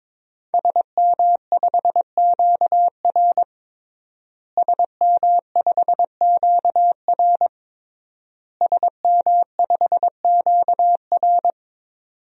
Morse code, heard as a keyed tone, about 22 wpm, 700 Hz. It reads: SM5QR SM5QR SM5QR
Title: DXCC-100 Call Signs - 5 Characters - Repeated 3x - 22wpm